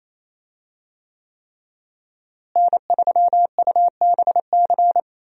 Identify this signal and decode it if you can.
Morse code, heard as a keyed tone, about 28 wpm, 700 Hz. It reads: N3UBC
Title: USA Call Signs - 5 Characters - Repeated 3x - 28wpm